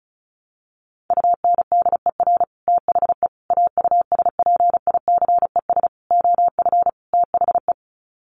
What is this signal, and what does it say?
Morse code, heard as a keyed tone, about 34 wpm, 700 Hz. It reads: UNDER THE AUSPICES OF THE
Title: Sets of 5 Words Encore2 35wpm